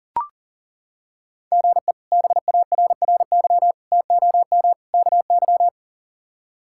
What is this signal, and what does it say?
Morse code, heard as a keyed tone, about 37 wpm, 700 Hz. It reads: GE BARRY TOM KY
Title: K1USN SST QSO Practice - 40wpm